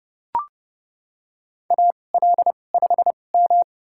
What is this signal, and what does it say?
Morse code, heard as a keyed tone, about 31 wpm, spaced as for 22 wpm, 700 Hz. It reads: AL5M
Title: USA Call Signs - 4 Characters - 2x Character Spacing - 30wpm